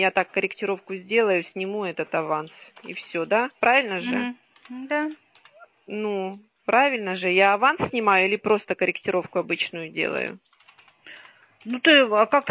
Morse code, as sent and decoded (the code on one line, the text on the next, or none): none